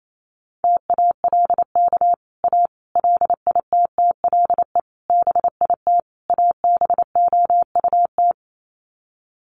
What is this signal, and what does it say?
Morse code, heard as a keyed tone, about 28 wpm, 700 Hz. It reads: TALK A LITTLE BIT ABOUT
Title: Sets of 5 Words Encore 28wpm